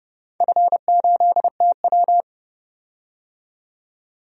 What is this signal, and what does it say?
Morse code, heard as a keyed tone, about 30 wpm, 700 Hz. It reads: F8TW
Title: DXCC-100 Call Signs - 4 Characters - 30wpm